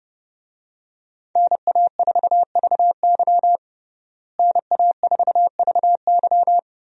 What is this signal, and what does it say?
Morse code, heard as a keyed tone, about 30 wpm, 700 Hz. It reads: NA4VY NA4VY
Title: USA Call Signs - 5 Characters - Repeated 2x - 30wpm